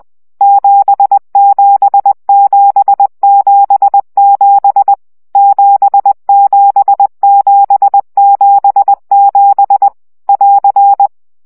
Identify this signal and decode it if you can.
Morse code, heard as a keyed tone, about 20 wpm, 800 Hz. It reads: 77777 77777 <AR>
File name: CW gyakorlas 7 17 baud